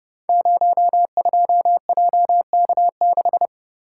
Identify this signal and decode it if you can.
Morse code, heard as a keyed tone, about 30 wpm, 700 Hz. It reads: 02JK6
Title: Code Groups - 5 Letter:Numbers - 30wpm